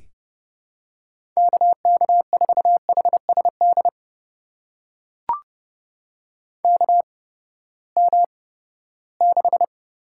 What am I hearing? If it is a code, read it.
Morse code, heard as a keyed tone, about 30 wpm, spaced as for 17 wpm, 700 Hz. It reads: KK4HSD KM6
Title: USA Call Signs - 8x Character Spacing - 30wpm